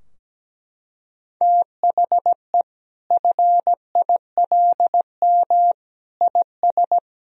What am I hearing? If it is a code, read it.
Morse code, heard as a keyed tone, about 17 wpm, 700 Hz. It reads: THE FILM IS
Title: Sentences from Top 1000 Words - Review Part 3 - 17wpm